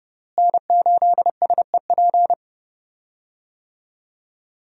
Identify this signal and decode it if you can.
Morse code, heard as a keyed tone, about 31 wpm, 700 Hz. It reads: N8SEP
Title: USA Call Signs - 5 Characters - Repeated 3x - 30wpm